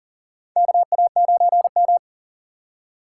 Morse code, heard as a keyed tone, about 40 wpm, 700 Hz. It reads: KA9M
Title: USA Call Signs - 2x Character Spacing - 40wpm